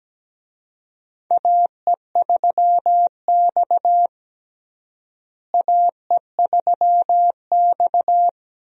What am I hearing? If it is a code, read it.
Morse code, heard as a keyed tone, about 17 wpm, 700 Hz. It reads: AE3X AE3X
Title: USA Call Signs - 4 Characters - Repeated 2x - 17wpm